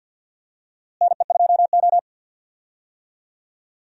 Morse code, heard as a keyed tone, about 52 wpm, 700 Hz. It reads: NE2O
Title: USA Call Signs - 4 Characters - Repeated 3x - 50wpm